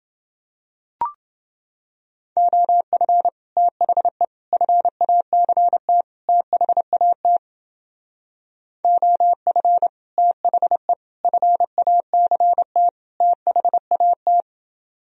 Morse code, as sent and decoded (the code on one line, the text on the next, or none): --- ..-. / - .... / ..-. .- -.-. - / - .... .- - / --- ..-. / - .... / ..-. .- -.-. - / - .... .- -
OF THE FACT THAT OF THE FACT THAT